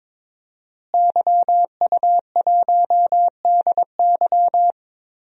Morse code, heard as a keyed tone, about 22 wpm, 700 Hz. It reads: YU1DY
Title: DXCC-100 Call Signs - 5 Characters - Repeated 3x - 22wpm